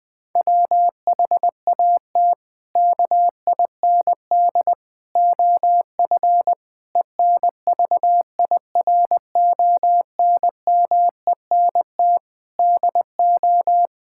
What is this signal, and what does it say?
Morse code, heard as a keyed tone, about 20 wpm, 700 Hz. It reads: WHAT KIND OF ENVIRONMENT DO
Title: Sentences from Top 1000 Words - Review Part 5 - 20wpm